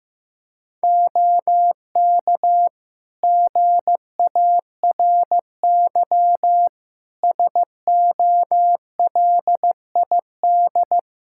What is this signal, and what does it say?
Morse code, heard as a keyed tone, about 15 wpm, 700 Hz. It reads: OK GARY SOLID
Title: Practice QSO - Exchange 3 - More Info - 15wpm